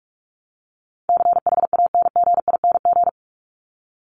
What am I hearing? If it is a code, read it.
Morse code, heard as a keyed tone, about 44 wpm, 700 Hz. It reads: CHANGING
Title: ING Suffix - Top 100 Words with Base Words 45wpm